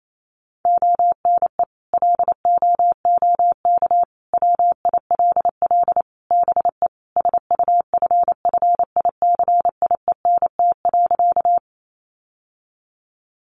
Morse code, heard as a keyed tone, about 28 wpm, 700 Hz. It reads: ONE LOOK WILL BE SUFFICIENT.
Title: Sentences from Top 1400 Words 28wpm